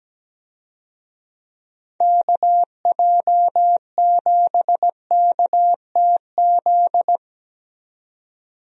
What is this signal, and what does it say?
Morse code, heard as a keyed tone, about 17 wpm, 700 Hz. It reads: KJ7KTZ
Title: USA Call Signs - 6 Characters - 17wpm